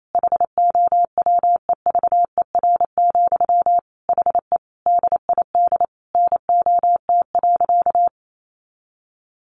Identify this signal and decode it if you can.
Morse code, heard as a keyed tone, about 28 wpm, 700 Hz. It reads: HOWEVER, HE DID NOT.
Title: Sentences from Top 600 Words 28wpm